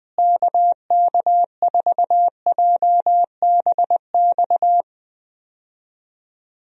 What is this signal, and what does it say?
Morse code, heard as a keyed tone, about 20 wpm, 700 Hz. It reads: KK4JBX
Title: USA Call Signs - 6 Characters - Repeated 2x - 20wpm